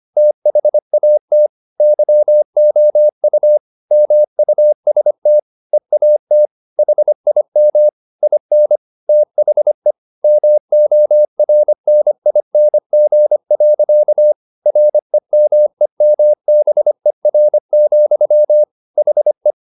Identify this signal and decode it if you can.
Morse code, heard as a keyed tone, about 25 wpm, 600 Hz. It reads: THAT YOU MUST EAT HIM IN THE MORNING. REMEMBER, HE
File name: the old man and the sea part 1 25wpm